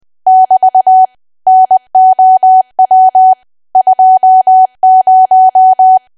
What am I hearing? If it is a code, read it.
Morse code, heard as a keyed tone, about 20 wpm, 750 Hz. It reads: <BT> NOW 20
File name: Archive / 20 WPM / 20WPM